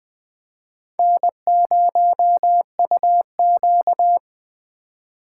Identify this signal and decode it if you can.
Morse code, heard as a keyed tone, about 21 wpm, 700 Hz. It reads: N0UQ